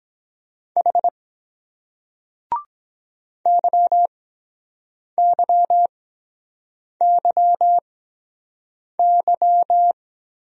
Morse code, heard as a keyed tone, about 23 wpm, spaced as for 7 wpm, 700 Hz. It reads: H YYYY